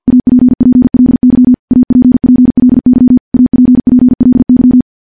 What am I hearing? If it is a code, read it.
Morse code, heard as a keyed tone, about 50 wpm, 250 Hz. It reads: APPLY APPLY APPLY
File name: morse / morse-training-50WPM